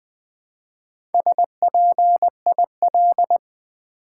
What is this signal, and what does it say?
Morse code, heard as a keyed tone, about 20 wpm, 700 Hz. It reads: SPIL
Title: Build A Word - 1 Letter At A Time - 5-Letter Words - 20wpm